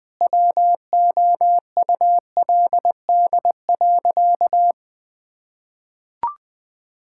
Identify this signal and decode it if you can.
Morse code, heard as a keyed tone, about 20 wpm, 700 Hz. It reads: WOULD.
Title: Sentences with 9-Word Phrases - 20wpm